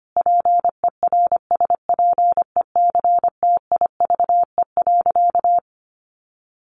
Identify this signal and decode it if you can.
Morse code, heard as a keyed tone, about 25 wpm, 700 Hz. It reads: PERSPECTIVE.
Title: Sentences from Top 1200 Words 25wpm